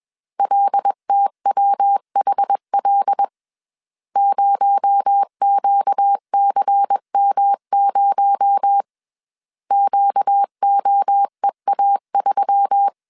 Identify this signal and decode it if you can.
Morse code, heard as a keyed tone, about 21 wpm, 800 Hz. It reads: LTWHL 0QCM0 QOEA3
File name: CW / DET / 1 6 1500C S C 20wpm 2116